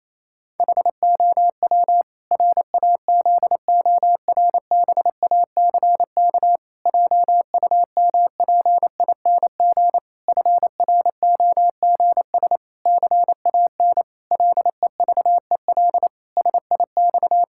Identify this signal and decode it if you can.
Morse code, heard as a keyed tone, about 28 wpm, 700 Hz. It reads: HOW RAZORBACK JUMPING FROGS CAN LEVEL SIX
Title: Pangram Challenge v2 - 28wpm